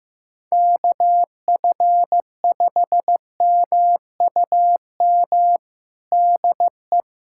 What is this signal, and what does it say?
Morse code, heard as a keyed tone, about 15 wpm, 700 Hz. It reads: KF5MUM DE
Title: USA Callsign DE Callsign - 6 Characters - 15wpm